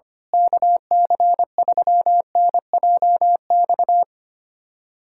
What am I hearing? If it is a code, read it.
Morse code, heard as a keyed tone, about 25 wpm, 700 Hz. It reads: KC3NJX
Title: USA Callsign DE Callsign - 6 Characters - 25wpm